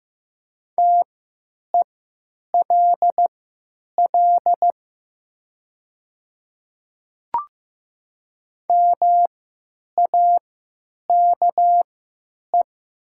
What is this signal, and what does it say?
Morse code, heard as a keyed tone, about 15 wpm, spaced as for 9 wpm, 700 Hz. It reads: TELL MAKE
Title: 3x Character Spacing - Top 200 Words - 15wpm